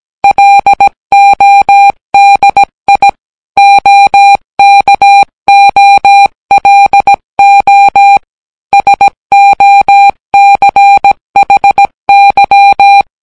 Morse code, heard as a keyed tone, about 17 wpm, 800 Hz. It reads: LODI OKOLO SOCHY